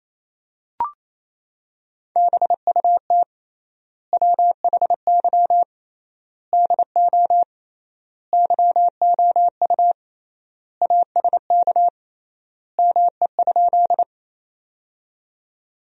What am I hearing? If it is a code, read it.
Morse code, heard as a keyed tone, about 27 wpm, 700 Hz. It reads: BUT WHY DO YOU ASK ME?
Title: 3x Word-Spacing - Sentences from Top 200 Words - 28wpm